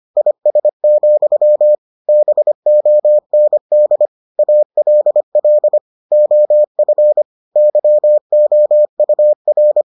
Morse code, heard as a keyed tone, about 25 wpm, 600 Hz. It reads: IS, BOND ALL OF YOUR